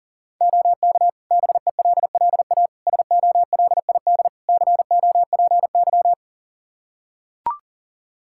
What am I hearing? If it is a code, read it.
Morse code, heard as a keyed tone, about 40 wpm, 700 Hz. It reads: OK BELLA SOLID COPY
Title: K1USN SST QSO Practice 40wpm